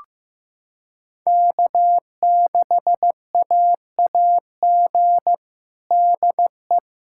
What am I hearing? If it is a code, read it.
Morse code, heard as a keyed tone, about 16 wpm, 700 Hz. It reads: K6AAG DE
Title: USA Callsign DE Callsign - 5 Characters - 15wpm